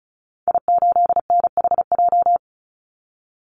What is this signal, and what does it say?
Morse code, heard as a keyed tone, about 34 wpm, 700 Hz. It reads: I8NHJ
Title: Call Signs - 35wpm